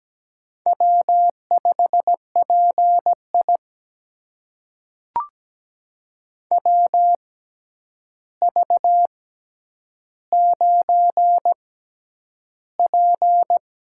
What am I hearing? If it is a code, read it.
Morse code, heard as a keyed tone, about 17 wpm, spaced as for 9 wpm, 700 Hz. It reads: W5PI WV9P
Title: USA Call Signs - 4 Characters - 6x Character Spacing - 17wpm